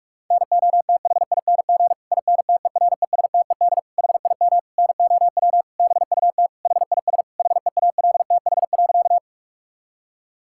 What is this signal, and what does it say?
Morse code, heard as a keyed tone, about 46 wpm, 700 Hz. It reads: NOTHING INTERESTED HIM NOW BUT HIS HEALTH.
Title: Sentences from Top 1000 Words - Review Part 6 - 45wpm